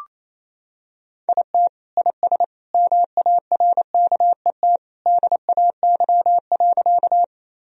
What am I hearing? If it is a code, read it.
Morse code, heard as a keyed tone, about 28 wpm, 700 Hz. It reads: IT IS MARKET DAY.
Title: Sentences from Top 1000 Words - Review Part 3 - 28wpm